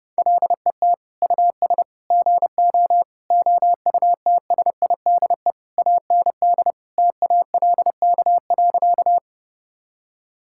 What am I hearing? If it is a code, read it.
Morse code, heard as a keyed tone, about 30 wpm, 700 Hz. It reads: LET US GO OUTSIDE AND TALK.
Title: Sentences from Top 1000 Words - Review Part 6 - 30wpm